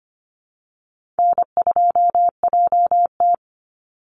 Morse code, heard as a keyed tone, about 25 wpm, 700 Hz. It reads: N2JT
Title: Easy Call Signs - 25wpm